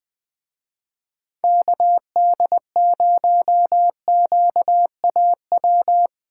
Morse code, heard as a keyed tone, about 20 wpm, 700 Hz. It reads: KD0QAW
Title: Speed Racing - USA Call Signs 17wpm